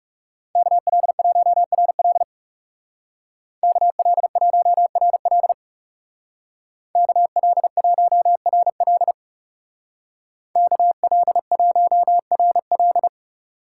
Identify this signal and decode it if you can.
Morse code, heard as a keyed tone, about 40 wpm, 700 Hz. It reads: KL1RL KL1RL KL1RL KL1RL